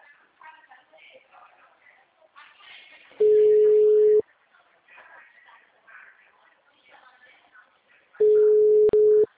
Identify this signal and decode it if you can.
Ringback tone (Europe, 425 Hz), with speech.